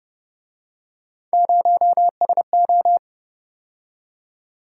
Morse code, heard as a keyed tone, about 30 wpm, 700 Hz. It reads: 0SO